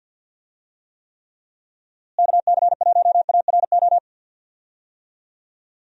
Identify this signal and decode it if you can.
Morse code, heard as a keyed tone, about 50 wpm, 700 Hz. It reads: KC1ARO
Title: USA Call Signs - 6 Characters - Repeated 3x - 50wpm